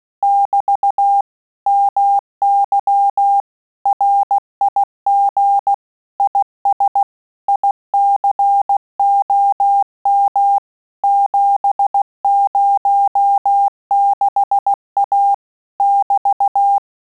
Morse code, heard as a keyed tone, about 16 wpm, 800 Hz. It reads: <BT> MY RIG IS ICOM 706A <BT>
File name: Telegraphy-group / QSO CW 15 PPM